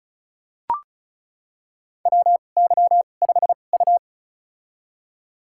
Morse code, heard as a keyed tone, about 36 wpm, spaced as for 25 wpm, 700 Hz. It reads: WY5U
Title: USA Call Signs - 2x Character Spacing - 35wpm